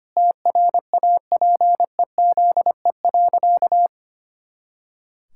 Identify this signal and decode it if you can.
Morse code, heard as a keyed tone, about 25 wpm, 700 Hz. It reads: TRAPEZE.
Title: Pangram Challenge v2 - 25wpm